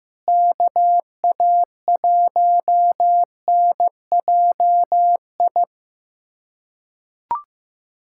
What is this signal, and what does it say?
Morse code, heard as a keyed tone, about 15 wpm, 700 Hz. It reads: KA1NJI
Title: USA Callsign DE Callsign - 6 Characters - 15wpm